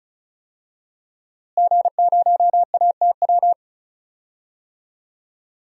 Morse code, heard as a keyed tone, about 36 wpm, 700 Hz. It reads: G0ATW
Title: DXCC-100 Call Signs - 5 Characters - Repeated 2x - 35wpm